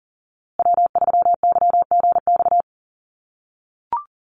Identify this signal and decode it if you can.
Morse code, heard as a keyed tone, about 40 wpm, 700 Hz. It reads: W3YGX
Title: USA Call Signs Repeated 3x 40wpm